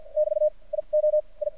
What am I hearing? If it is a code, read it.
Morse code, heard as a keyed tone, about 52 wpm, spaced as for 38 wpm, 610 Hz. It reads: <BT> IO I
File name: archived / CRSnet